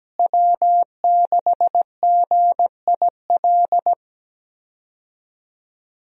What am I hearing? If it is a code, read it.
Morse code, heard as a keyed tone, about 17 wpm, 700 Hz. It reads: W6GIL
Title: USA Call Signs - 5 Characters - Repeated 2x - 17wpm